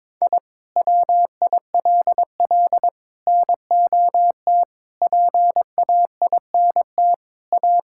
Morse code, heard as a keyed tone, about 22 wpm, 700 Hz. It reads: I WILL NOT PAINT A
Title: Sentences from Top 1000 Words - Review Part 6 - 22wpm